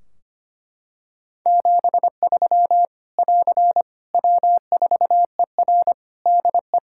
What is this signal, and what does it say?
Morse code, heard as a keyed tone, about 25 wpm, 700 Hz. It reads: 73 <AR> W4ER DE
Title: Practice QSO - Exchange 4 - Ending - 25wpm